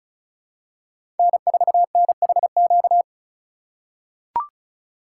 Morse code, heard as a keyed tone, about 36 wpm, 700 Hz. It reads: N4NHQ
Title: USA Call Signs - 5 Characters - 2x Character Spacing - 35wpm